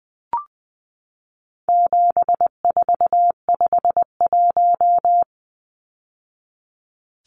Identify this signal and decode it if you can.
Morse code, heard as a keyed tone, about 20 wpm, 700 Hz. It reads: 7451
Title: Numbers - 4-Digit 20wpm